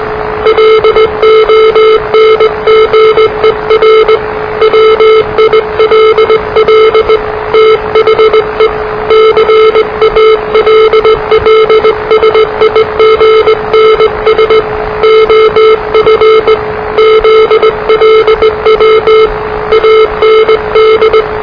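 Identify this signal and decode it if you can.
Morse code, heard as a keyed tone, about 20 wpm, 440 Hz, with noise. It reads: LONGER WILL THE CALLSIGNS OF ZLW AND
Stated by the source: Morse challenge from NZ Net, the national amateur radio net of Aotearoa/New Zealand, which meets on 3535.0 kHz.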